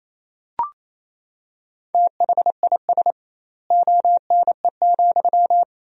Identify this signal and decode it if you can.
Morse code, heard as a keyed tone, about 28 wpm, 700 Hz. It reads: THIS ONE,
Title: 2x Word-Spacing - Sentences from Top 200 Words - 28wpm